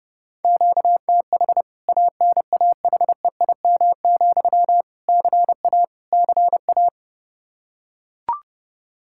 Morse code, heard as a keyed tone, about 30 wpm, 700 Hz. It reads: QTH ANAHEIM, CA CA
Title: K1USN SST QSO Practice 30wpm